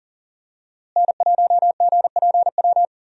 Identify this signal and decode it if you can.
Morse code, heard as a keyed tone, about 40 wpm, 700 Hz. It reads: N1GPW
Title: USA Call Signs - 5 Characters - Repeated 2x - 40wpm